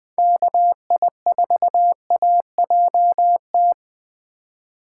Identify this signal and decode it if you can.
Morse code, heard as a keyed tone, about 21 wpm, 700 Hz. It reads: KI4AJT
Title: Speed Racing - USA Call Signs 17wpm